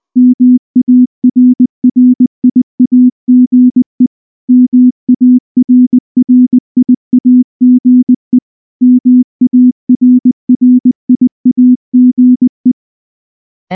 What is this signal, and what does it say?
Morse code, heard as a keyed tone, about 19 wpm, 250 Hz. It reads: MARRIAGE MARRIAGE MARRIAGE
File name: morse / morse-training-20WPM-6